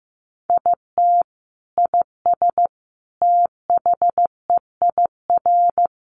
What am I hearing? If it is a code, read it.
Morse code, heard as a keyed tone, about 15 wpm, 700 Hz. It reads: IT IS THEIR
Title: Sentences from Top 1000 Words 15wpm